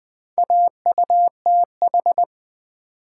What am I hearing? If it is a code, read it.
Morse code, heard as a keyed tone, about 20 wpm, 700 Hz. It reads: AUTH
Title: Build A Word - 2 Letters At A Time - 6-Letter Words - 20wpm